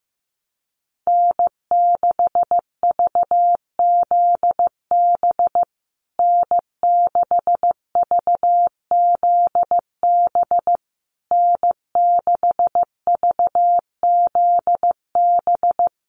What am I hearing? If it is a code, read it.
Morse code, heard as a keyed tone, about 15 wpm, 700 Hz. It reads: N6VZB N6VZB N6VZB
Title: USA Call Signs Repeated 3x 15wpm